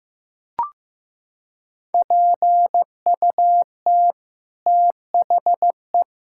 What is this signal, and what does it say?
Morse code, heard as a keyed tone, about 15 wpm, 700 Hz. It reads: PUT THE
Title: Sentences from Top 1600 Words 15wpm